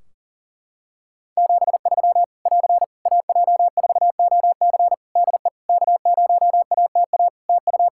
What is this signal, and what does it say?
Morse code, heard as a keyed tone, about 40 wpm, 700 Hz. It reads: 73 <AR> AJ4OC DE K0ATA TU
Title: Practice QSO - Exchange 4 - Ending - 40wpm